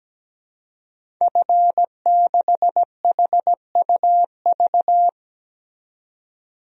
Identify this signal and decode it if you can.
Morse code, heard as a keyed tone, about 17 wpm, 700 Hz. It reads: F6HUV
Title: DXCC-100 Call Signs - 5 Characters - 17wpm